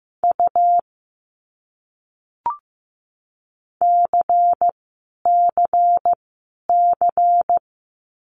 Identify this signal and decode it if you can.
Morse code, heard as a keyed tone, about 15 wpm, 700 Hz. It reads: U C C C
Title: Abbreviations Repeated 3x - 15wpm